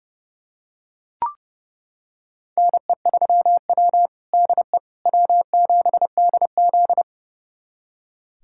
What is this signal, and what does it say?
Morse code, heard as a keyed tone, about 32 wpm, 700 Hz. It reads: NE3W DE W7DZ
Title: USA Callsign DE Callsign - 4 Characters - 30wpm